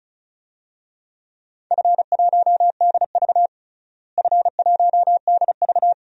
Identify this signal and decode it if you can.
Morse code, heard as a keyed tone, about 36 wpm, 700 Hz. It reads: F1DV F1DV
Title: DXCC-100 Call Signs - 4 Characters - Repeated 2x - 35wpm